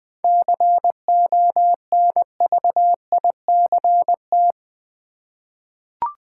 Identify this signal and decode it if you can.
Morse code, heard as a keyed tone, about 20 wpm, 700 Hz. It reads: CONVICT
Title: 7 Letter Words - First 3 Letters Given - 20wpm